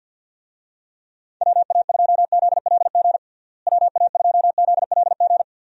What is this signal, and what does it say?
Morse code, heard as a keyed tone, about 46 wpm, 700 Hz. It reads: WA2ZLG WA2ZLG
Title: USA Call Signs - 6 Characters - Repeated 3x - 50wpm